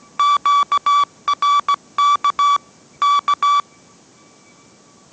Morse code, heard as a keyed tone, about 17 wpm, 1150 Hz. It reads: QRK K